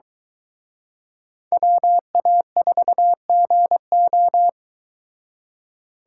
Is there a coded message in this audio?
Yes: Morse code from 1.5 to 4.5 s, heard as a keyed tone, about 23 wpm, 700 Hz. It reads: WA4GO